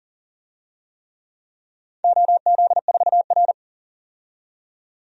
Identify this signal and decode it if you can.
Morse code, heard as a keyed tone, about 40 wpm, 700 Hz. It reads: OZ4R
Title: DXCC-100 Call Signs - 4 Characters - 40wpm